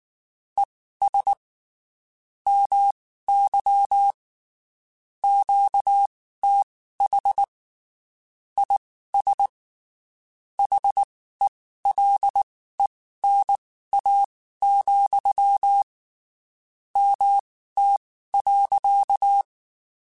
Morse code, heard as a keyed tone, about 19 wpm, spaced as for 14 wpm, 800 Hz. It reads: ES MY QTH IS HELENA, MT.
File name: (spaced) / QSOs Spaced-mp3s / WD0BGP 1 @18 sp